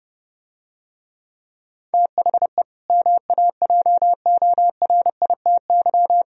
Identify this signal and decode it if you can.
Morse code, heard as a keyed tone, about 30 wpm, 700 Hz. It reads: THE MAJORITY